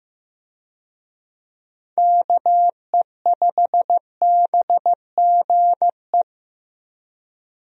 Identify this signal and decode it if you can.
Morse code, heard as a keyed tone, about 15 wpm, 700 Hz. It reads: KE5BGE
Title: USA Call Signs - 6 Characters - Repeated 2x - 15wpm